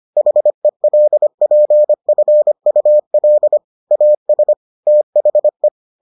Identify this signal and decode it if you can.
Morse code, heard as a keyed tone, about 25 wpm, 600 Hz. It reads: HELPFUL AS THE